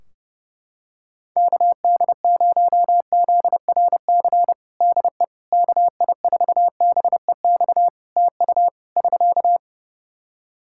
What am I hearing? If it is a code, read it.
Morse code, heard as a keyed tone, about 31 wpm, 700 Hz. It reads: KD0ZRC DE KI4BEX TU <SK>
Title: Practice QSO - Exchange 4 - Ending - 30wpm